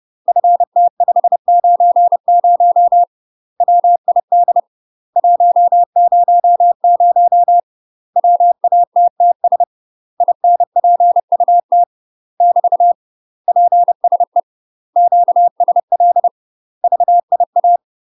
Morse code, heard as a keyed tone, about 30 wpm, 700 Hz. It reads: FT590 WID 100 WATTS INPUT <BT> PSE QSL VIA
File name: UK MASTER QSOs TRAINING / 30WPM / UK MASTER QSOs 0028